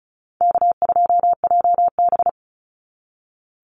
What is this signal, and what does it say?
Morse code, heard as a keyed tone, about 36 wpm, 700 Hz. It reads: K2JB
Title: NA SOTA Chaser Callsigns 35wpm